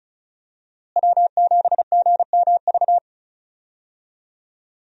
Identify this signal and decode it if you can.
Morse code, heard as a keyed tone, about 36 wpm, 700 Hz. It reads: W7GMV
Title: USA Call Signs - 5 Characters - Repeated 2x - 35wpm